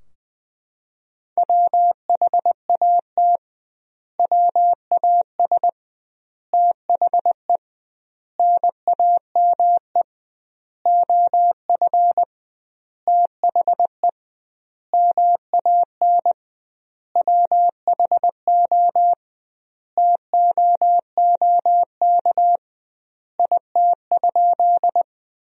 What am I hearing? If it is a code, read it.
Morse code, heard as a keyed tone, about 21 wpm, 700 Hz. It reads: WHAT WAS THE NAME OF THE MAN WHO TOOK IT?
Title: 2x Word-Spacing - Sentences from Top 300 Words - 20wpm